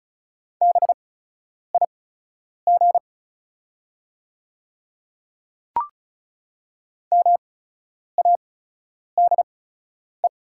Morse code, heard as a keyed tone, about 34 wpm, spaced as for 10 wpm, 700 Hz. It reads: BIG MADE
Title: 8x Character Spacing - Top 200 Words - 35wpm